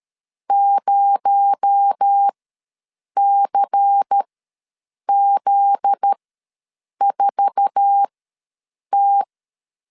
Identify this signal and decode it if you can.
Morse code, heard as a keyed tone, about 13 wpm, spaced as for 7 wpm, 800 Hz. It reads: 0CZ4T
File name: LEZIONI / 9 286 ws 12wpm 3351